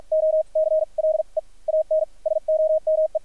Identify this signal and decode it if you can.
Morse code, heard as a keyed tone, about 53 wpm, spaced as for 38 wpm, 590 Hz, with radio noise. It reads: O CLEAN SOME
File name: archived / CRSnet